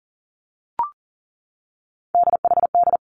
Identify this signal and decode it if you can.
Morse code, heard as a keyed tone, about 38 wpm, 700 Hz. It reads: DHD
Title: Three Challenging Letter-Number Combinations 40wpm